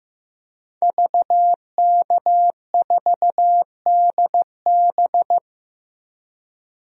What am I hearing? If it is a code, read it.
Morse code, heard as a keyed tone, about 15 wpm, 700 Hz. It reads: VK4DB
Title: DXCC-100 Call Signs - 5 Characters - Repeated 3x - 15wpm